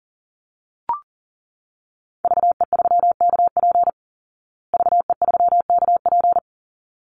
Morse code, heard as a keyed tone, about 40 wpm, 700 Hz. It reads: VE3KP VE3KP
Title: Call Signs Repeated 3x - 40wpm